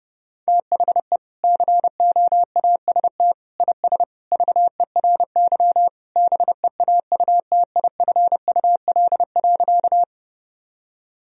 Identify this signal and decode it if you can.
Morse code, heard as a keyed tone, about 30 wpm, 700 Hz. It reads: THE COAST IS VERY BEAUTIFUL.
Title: Sentences from Top 1000 Words - Review Part 4 - 30wpm